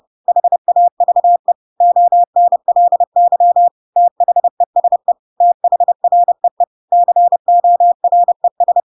Morse code, heard as a keyed tone, about 30 wpm, 700 Hz. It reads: HAVE ONLY THESE THREE CORES
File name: Pinocchio / Pinocchio ch7 30 30 ik1hge